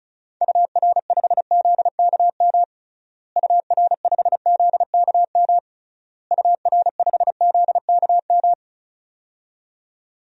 Morse code, heard as a keyed tone, about 36 wpm, 700 Hz. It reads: UR5ZKM UR5ZKM UR5ZKM
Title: DXCC-100 Call Signs - 6 Characters - Repeated 3x - 35wpm